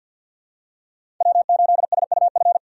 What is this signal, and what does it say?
Morse code, heard as a keyed tone, about 48 wpm, 700 Hz. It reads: W8SUF
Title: USA Call Signs - 5 Characters - 6x Character Spacing - 50wpm